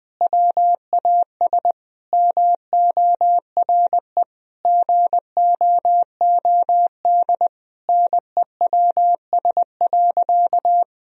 Morse code, heard as a keyed tone, about 20 wpm, 700 Hz. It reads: WAS MORE GOOD NEWS.
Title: Sentences from Top 1000 Words - Review Part 1 - 20wpm